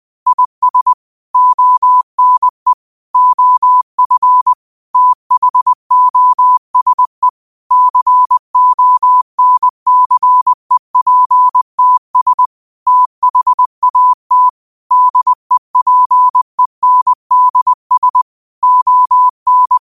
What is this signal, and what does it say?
Morse code, heard as a keyed tone, about 20 wpm, 1000 Hz. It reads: IS ONE OF THOSE CONCEPTS THAT DEPENDS ON